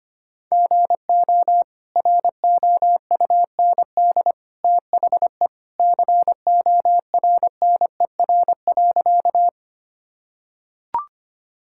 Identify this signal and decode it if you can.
Morse code, heard as a keyed tone, about 25 wpm, 700 Hz. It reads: GO ROUND THE CORNER.
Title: Sentences from Top 1600 Words 25wpm